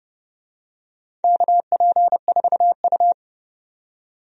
Morse code, heard as a keyed tone, about 32 wpm, 700 Hz. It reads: KP4U
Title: DXCC-100 Call Signs - 4 Characters - 30wpm